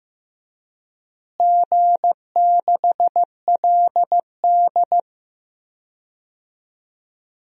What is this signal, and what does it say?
Morse code, heard as a keyed tone, about 15 wpm, 700 Hz. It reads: G6LD
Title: DXCC-100 Call Signs - 4 Characters - Repeated 2x - 15wpm